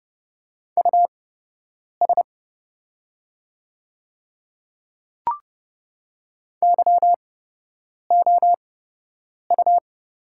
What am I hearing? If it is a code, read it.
Morse code, heard as a keyed tone, about 30 wpm, spaced as for 8 wpm, 700 Hz. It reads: US YOU
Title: 8x Character Spacing - Top 200 Words - 30wpm